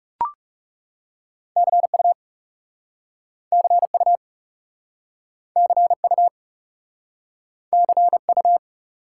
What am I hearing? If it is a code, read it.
Morse code, heard as a keyed tone, about 39 wpm, 700 Hz. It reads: CU CU CU CU